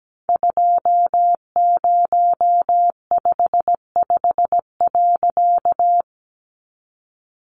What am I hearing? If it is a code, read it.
Morse code, heard as a keyed tone, about 17 wpm, 700 Hz. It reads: 2055.
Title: A Sound Of Thunder 17wpm